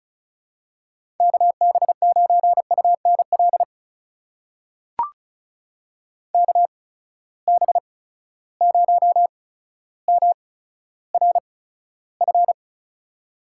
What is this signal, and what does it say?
Morse code, heard as a keyed tone, about 36 wpm, spaced as for 15 wpm, 700 Hz. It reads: KB9UNL KB0MRF
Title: USA Call Signs - 6 Characters - 8x Character Spacing - 35wpm